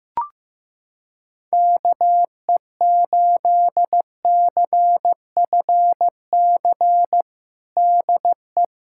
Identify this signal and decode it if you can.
Morse code, heard as a keyed tone, about 15 wpm, 700 Hz. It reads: KE8CFC DE
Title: USA Callsign DE Callsign - 6 Characters - 15wpm